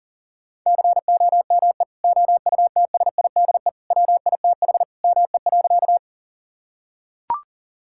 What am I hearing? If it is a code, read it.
Morse code, heard as a keyed tone, about 37 wpm, 700 Hz. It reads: COME OUTSIDE WITH ME.
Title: Sentences from Top 1000 Words - Review Part 6 - 40wpm